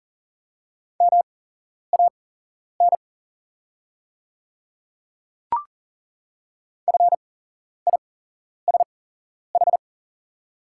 Morse code, heard as a keyed tone, about 40 wpm, spaced as for 11 wpm, 690 Hz. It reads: MAN FISH
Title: 8x Character Spacing - Top 300 Words - 40wpm